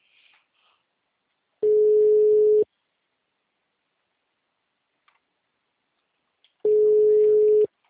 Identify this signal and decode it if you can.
Ringback tone (Europe, 425 Hz).